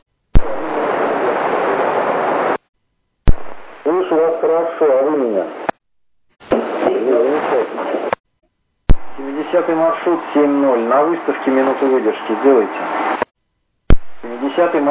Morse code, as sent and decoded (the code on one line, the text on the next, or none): none